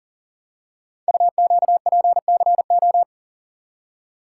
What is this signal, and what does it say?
Morse code, heard as a keyed tone, about 41 wpm, 690 Hz. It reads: UQPCO
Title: Code Groups - 5 Letters - 40wpm